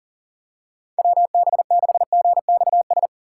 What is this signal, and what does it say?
Morse code, heard as a keyed tone, about 40 wpm, 700 Hz. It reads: WB6GXS
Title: USA Call Signs - 6 Characters - 3x Character Spacing - 40wpm